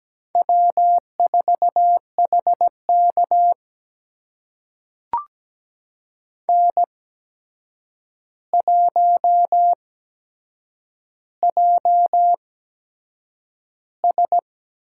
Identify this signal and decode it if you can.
Morse code, heard as a keyed tone, about 17 wpm, spaced as for 7 wpm, 700 Hz. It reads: W4HK N1JS
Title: USA Call Signs - 8x Character Spacing - 17wpm